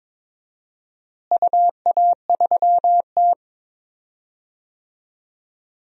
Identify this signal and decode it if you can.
Morse code, heard as a keyed tone, about 22 wpm, 700 Hz. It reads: UA3T
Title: DXCC-100 Call Signs - 4 Characters - Repeated 3x - 22wpm